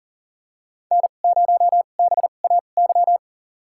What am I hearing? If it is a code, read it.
Morse code, heard as a keyed tone, about 40 wpm, spaced as for 29 wpm, 700 Hz. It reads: N0BAY
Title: USA Call Signs - 5 Characters - 2x Character Spacing - 40wpm